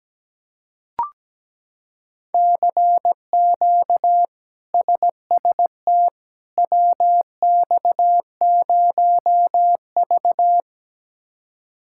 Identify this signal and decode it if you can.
Morse code, heard as a keyed tone, about 17 wpm, 700 Hz. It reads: CQ SST WX0V
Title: K1USN SST QSO Practice - 17wpm